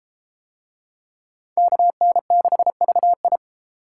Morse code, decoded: KN6VI

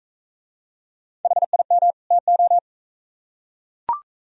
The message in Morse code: HIM TO

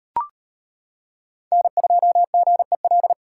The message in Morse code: N2GEL

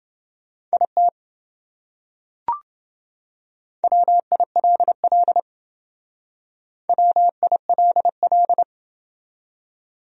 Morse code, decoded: IT WILL WILL

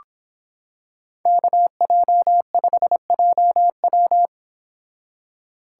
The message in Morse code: KJ5JW

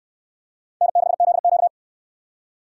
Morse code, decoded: N6ZQ

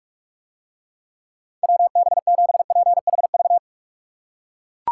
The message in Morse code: WB7PHV

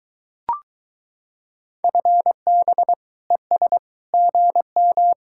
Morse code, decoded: FB ES GM

